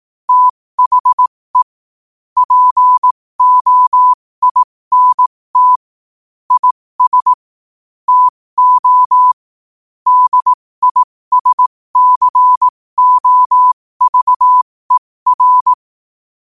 Morse code, heard as a keyed tone, about 18 wpm, spaced as for 15 wpm, 1000 Hz. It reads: THE POINT IS TO DISCOVER
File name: qotdfiles / 15wpm